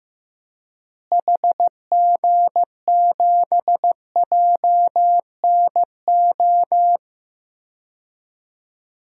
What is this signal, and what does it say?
Morse code, heard as a keyed tone, about 15 wpm, 700 Hz. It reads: HG7JNO
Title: DXCC-100 Call Signs - 6 Characters - Repeated 2x - 15wpm